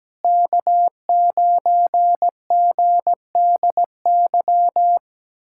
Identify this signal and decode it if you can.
Morse code, heard as a keyed tone, about 17 wpm, 700 Hz. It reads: K9GDY
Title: USA Callsign DE Callsign 17wpm